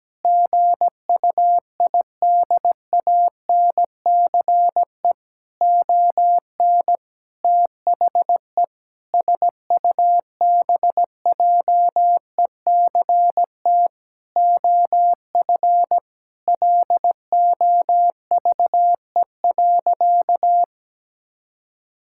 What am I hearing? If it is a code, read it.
Morse code, heard as a keyed tone, about 17 wpm, 700 Hz. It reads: GUIDANCE ON THE SUBJECT OF LOVE.
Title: Sentences from Top 1600 Words 17wpm